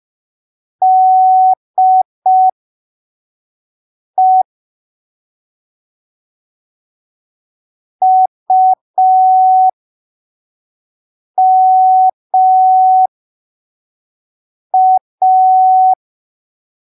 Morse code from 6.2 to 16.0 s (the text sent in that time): UMA